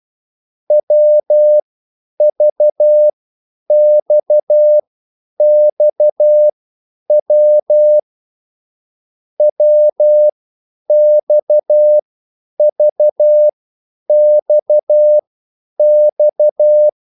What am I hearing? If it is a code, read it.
Morse code, heard as a keyed tone, about 12 wpm, spaced as for 9 wpm, 600 Hz. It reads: WVXXW WXVXX
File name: cwbook